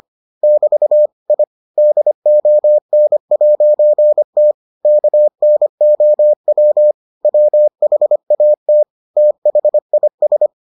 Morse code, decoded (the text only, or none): <BT> I DON'T KNOW WHAT THIS